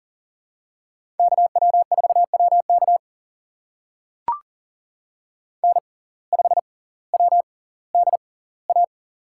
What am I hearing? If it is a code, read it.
Morse code, heard as a keyed tone, about 40 wpm, spaced as for 20 wpm, 700 Hz. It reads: KW4WK N5WDA
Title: USA Call Signs - 6x Character Spacing - 40wpm